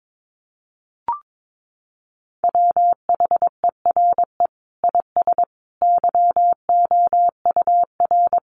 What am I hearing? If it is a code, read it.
Morse code, heard as a keyed tone, about 22 wpm, 700 Hz. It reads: WHERE IS YOUR
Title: Sentences from Top 1400 Words 22wpm